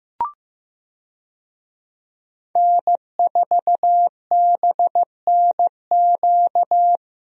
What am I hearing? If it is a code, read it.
Morse code, heard as a keyed tone, about 16 wpm, 700 Hz. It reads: N4BNQ
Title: USA Call Signs - 5 Characters - Repeated 2x - 15wpm